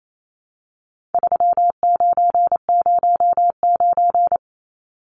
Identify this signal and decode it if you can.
Morse code, heard as a keyed tone, about 28 wpm, 700 Hz. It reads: 3909
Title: Numbers - 4-Digit 28wpm